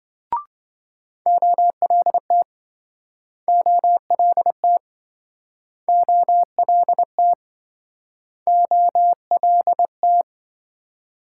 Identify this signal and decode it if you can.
Morse code, heard as a keyed tone, about 28 wpm, 700 Hz. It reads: OLT OLT OLT OLT